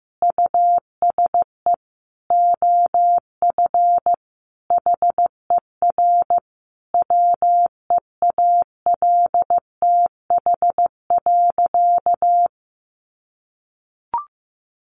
Morse code, heard as a keyed tone, about 15 wpm, 700 Hz. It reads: USE OF HER WEALTH.
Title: Sentences from Top 1500 Words 15wpm